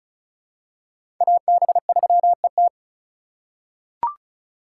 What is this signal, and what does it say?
Morse code, heard as a keyed tone, about 36 wpm, 700 Hz. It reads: AB3ET
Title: USA Call Signs - 3x Character Spacing - 35wpm